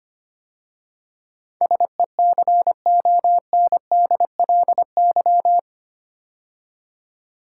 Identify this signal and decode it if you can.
Morse code, heard as a keyed tone, about 25 wpm, 700 Hz. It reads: SECONDLY
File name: 8 Letter Words - First 3 Letters Given - 25wpm